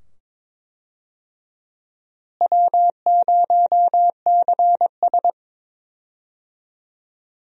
Morse code, heard as a keyed tone, about 22 wpm, 700 Hz. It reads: W0CS
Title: USA Call Signs - 4 Characters - Repeated 2x - 22wpm